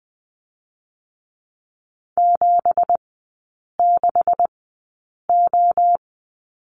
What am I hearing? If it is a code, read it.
Morse code, heard as a keyed tone, about 20 wpm, 700 Hz. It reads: 7 6 O
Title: 218 Lesson 35e - Number 0 - Three Characters 20wpm